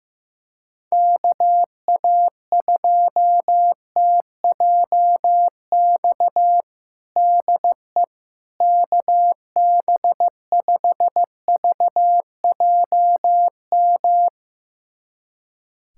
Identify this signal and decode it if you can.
Morse code, heard as a keyed tone, about 15 wpm, 700 Hz. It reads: KA2TJX DE KB5VJM
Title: USA Callsign DE Callsign 15wpm